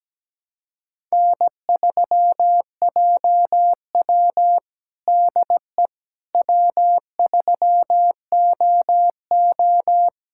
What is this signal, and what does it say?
Morse code, heard as a keyed tone, about 17 wpm, 700 Hz. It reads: N3JW DE W3OO
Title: USA Callsign DE Callsign - 4 Characters - 17wpm